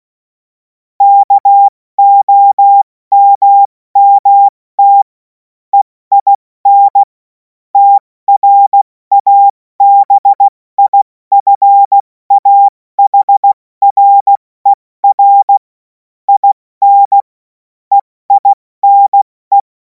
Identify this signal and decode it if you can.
Morse code, heard as a keyed tone, about 16 wpm, 800 Hz. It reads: KOMMT EIN TRABIFAHRER IN EINE
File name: Morsekurs / Witze / Witz 21. April 2026 70 F80